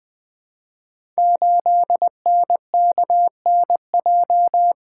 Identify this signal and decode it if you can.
Morse code, heard as a keyed tone, about 21 wpm, 700 Hz. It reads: 8NKNJ
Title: Code Groups - 5 Letter:Numbers - 20wpm